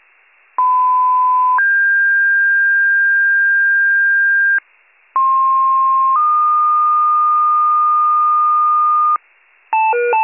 Two-tone page, heard as a one-second tone, then a three-second tone, about 1010 then 1600 Hz, about 1060 then 1220 Hz, with noise. It is heard from 0.6 to 9.2 s.